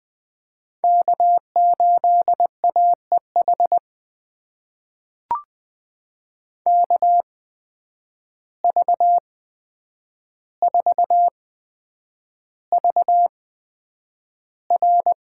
Morse code, heard as a keyed tone, about 20 wpm, spaced as for 9 wpm, 700 Hz. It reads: K8AEH KV4VR
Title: USA Call Signs - 8x Character Spacing - 20wpm